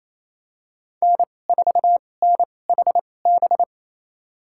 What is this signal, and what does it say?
Morse code, heard as a keyed tone, about 28 wpm, spaced as for 20 wpm, 700 Hz. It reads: N4NHB
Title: USA Call Signs - 5 Characters - 2x Character Spacing - 28wpm